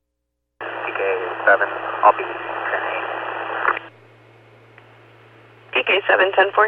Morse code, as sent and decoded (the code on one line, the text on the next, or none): none